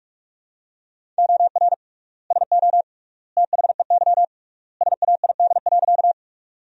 Morse code, heard as a keyed tone, about 46 wpm, 700 Hz. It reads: OR SO THEY SAID.
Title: 3x Word-Spacing - Sentences from Top 100 Words - 45wpm